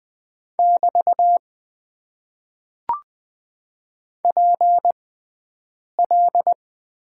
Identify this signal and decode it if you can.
Morse code, heard as a keyed tone, about 20 wpm, spaced as for 7 wpm, 700 Hz. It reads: <BT> PL
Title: Top 90 QSO Elements - 6x Character Spacing - 20wpm